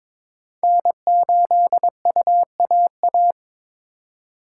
Morse code, heard as a keyed tone, about 26 wpm, 700 Hz. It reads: N8UAA
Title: Speed Racing - USA 1x3 Call Signs 22wpm